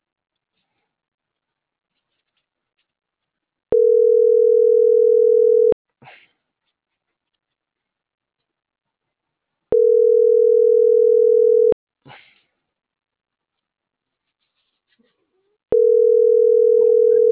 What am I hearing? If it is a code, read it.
Ringback tone (North America), 440 + 480 Hz.